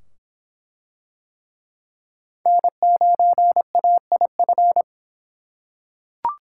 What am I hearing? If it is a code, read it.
Morse code, heard as a keyed tone, about 22 wpm, 700 Hz. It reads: N9AIF